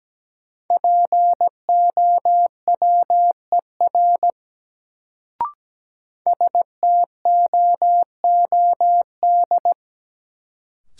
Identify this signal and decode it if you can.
Morse code, heard as a keyed tone, about 17 wpm, 700 Hz. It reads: POWER STOOD
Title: Sentences from Top 500 Words - New Word Drill - 17wpm